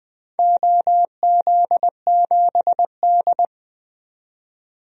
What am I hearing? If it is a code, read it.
Morse code, heard as a keyed tone, about 20 wpm, 700 Hz. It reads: OZ7D
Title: DXCC-100 Call Signs - 4 Characters - 20wpm